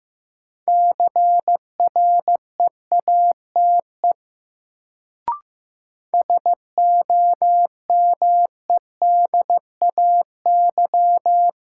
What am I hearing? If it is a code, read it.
Morse code, heard as a keyed tone, about 15 wpm, 700 Hz. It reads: CREATE SOMEDAY
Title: Sentences from Top 1600 Words - New Word Drill Alt - 15wpm